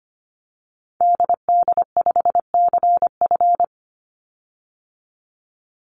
Morse code, heard as a keyed tone, about 25 wpm, 700 Hz. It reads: DD5CF
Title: Call Signs - 25wpm